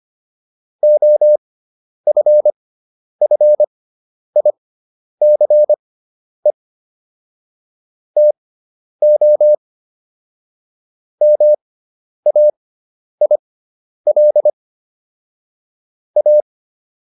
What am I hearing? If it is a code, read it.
Morse code, heard as a keyed tone, about 25 wpm, spaced as for 10 wpm, 600 Hz. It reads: OFFICE TO MAIL A